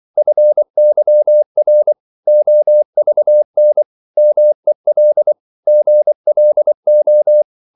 Morse code, heard as a keyed tone, about 24 wpm, 600 Hz. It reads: FYR OVN MEL GLO
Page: Korte danske ord 24wpm